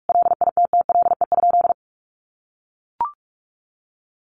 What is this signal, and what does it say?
Morse code, heard as a keyed tone, about 44 wpm, 700 Hz. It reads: LITTLE?
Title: Sentences from Top 1400 Words 45wpm